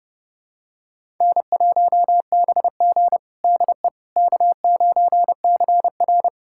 Morse code, heard as a keyed tone, about 30 wpm, 700 Hz. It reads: N1BG DE K9CR
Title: USA Callsign DE Callsign - 4 Characters - 30wpm